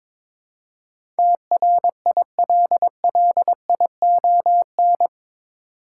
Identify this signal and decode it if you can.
Morse code, heard as a keyed tone, about 22 wpm, 700 Hz. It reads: TRILLION
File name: practice / 22wpm / Eight Letter Words - 22wpm